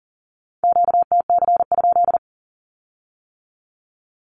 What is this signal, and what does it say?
Morse code, heard as a keyed tone, about 42 wpm, 700 Hz. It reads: QTC?
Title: Q-Codes with Spoken Meaning - 40wpm